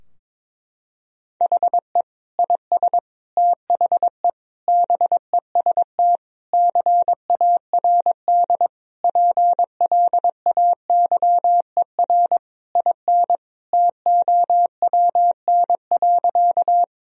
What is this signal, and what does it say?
Morse code, heard as a keyed tone, about 22 wpm, 700 Hz. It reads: HE IS THE BEST CARD PLAYER IN TOWN.
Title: Sentences from Top 1000 Words - Review Part 4 - 22wpm